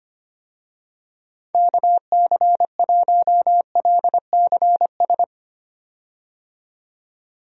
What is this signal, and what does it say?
Morse code, heard as a keyed tone, about 25 wpm, 700 Hz. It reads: KC1LCS
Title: USA Call Signs - 6 Characters - 25wpm